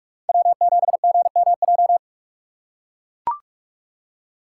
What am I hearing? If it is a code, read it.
Morse code, heard as a keyed tone, about 44 wpm, 700 Hz. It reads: W7GMJ
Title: USA Call Signs - 5 Characters - 4x Character Spacing - 45wpm